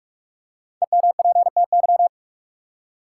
Morse code, heard as a keyed tone, about 46 wpm, 700 Hz. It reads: EMPTY
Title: Five Letter Words - 45wpm